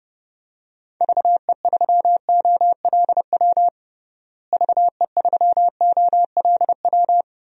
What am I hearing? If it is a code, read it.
Morse code, heard as a keyed tone, about 30 wpm, 700 Hz. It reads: VE3OLW VE3OLW
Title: DXCC-100 Call Signs - 6 Characters - Repeated 2x - 30wpm